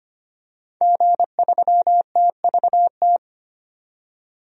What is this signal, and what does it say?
Morse code, heard as a keyed tone, about 25 wpm, 700 Hz. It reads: G3TVT
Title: DXCC-100 Call Signs - 5 Characters - 25wpm